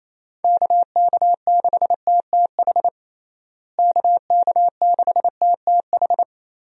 Morse code, heard as a keyed tone, about 28 wpm, 700 Hz. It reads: KK6TTH KK6TTH
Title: USA Call Signs - 6 Characters - Repeated 2x - 28wpm